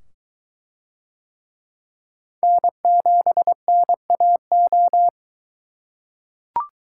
Morse code, heard as a keyed tone, about 21 wpm, 700 Hz. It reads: N7NAO